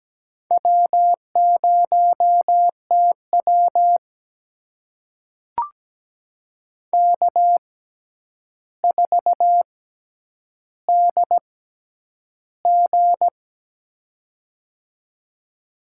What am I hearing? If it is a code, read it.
Morse code, heard as a keyed tone, about 17 wpm, spaced as for 9 wpm, 700 Hz. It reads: W0TW K4DG
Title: USA Call Signs - 6x Character Spacing - 17wpm